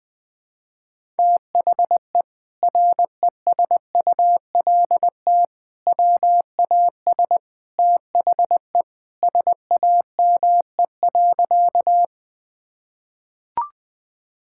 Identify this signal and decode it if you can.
Morse code, heard as a keyed tone, about 20 wpm, 700 Hz. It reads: THE RESULT WAS THE SAME.
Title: Sentences from Top 1000 Words - Review Part 5 - 20wpm